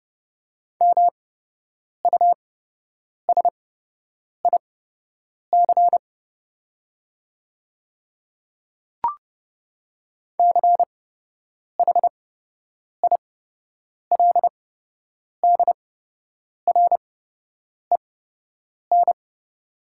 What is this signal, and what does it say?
Morse code, heard as a keyed tone, about 30 wpm, spaced as for 8 wpm, 700 Hz. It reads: MUSIC CHILDREN
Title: 8x Character Spacing - Top 300 Words - 30wpm